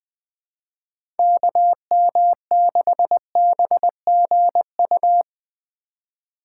Morse code, heard as a keyed tone, about 21 wpm, 700 Hz. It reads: KM6BGU
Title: USA Call Signs - 6 Characters - 2x Character Spacing - 20wpm